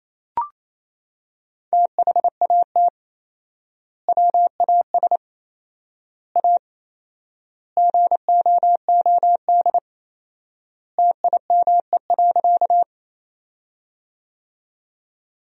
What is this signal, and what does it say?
Morse code, heard as a keyed tone, about 28 wpm, 700 Hz. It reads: THAT WAS A GOOD TIME.
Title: 4x Word-Spacing - Sentences from Top 200 Words - 28wpm